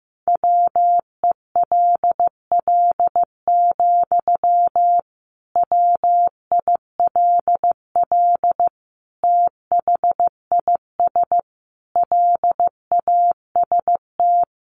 Morse code, heard as a keyed tone, about 15 wpm, 700 Hz. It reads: WELL, WILL THIS LAST
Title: Sentences from Top 1300 Words 15wpm